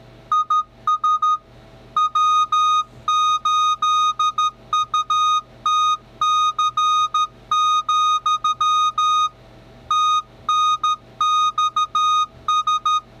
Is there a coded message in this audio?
Yes: Morse code from 0.3 to 13.0 s, heard as a keyed tone, about 13 wpm, 1240 Hz. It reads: IS W8UTC, TNXS